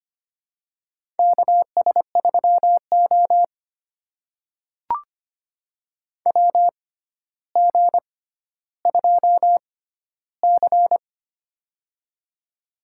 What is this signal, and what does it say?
Morse code, heard as a keyed tone, about 25 wpm, spaced as for 13 wpm, 700 Hz. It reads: KS3O WG2C
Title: USA Call Signs - 4 Characters - 6x Character Spacing - 25wpm